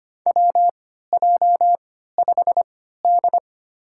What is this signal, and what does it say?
Morse code, heard as a keyed tone, about 25 wpm, spaced as for 14 wpm, 700 Hz. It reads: WJ5D
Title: USA Call Signs - 4 Characters - 3x Character Spacing - 25wpm